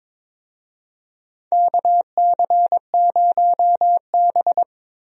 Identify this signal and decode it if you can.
Morse code, heard as a keyed tone, about 22 wpm, 700 Hz. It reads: KC0B
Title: USA Call Signs - 4 Characters - Repeated 2x - 22wpm